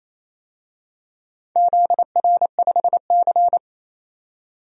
Morse code, heard as a keyed tone, about 28 wpm, 700 Hz. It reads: ZR5C
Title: DXCC-100 Call Signs - 4 Characters - 28wpm